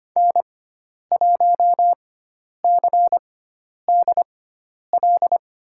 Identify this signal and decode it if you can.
Morse code, heard as a keyed tone, about 25 wpm, spaced as for 10 wpm, 700 Hz. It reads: N1CDL
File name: Lesson 16g - Letter c - Call Signs 25wpm